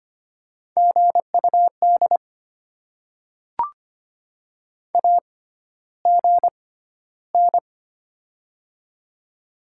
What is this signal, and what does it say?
Morse code, heard as a keyed tone, about 25 wpm, spaced as for 13 wpm, 700 Hz. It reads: GUD AGN
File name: Top 20 QSO Elements - 6x Character Spacing - 25wpm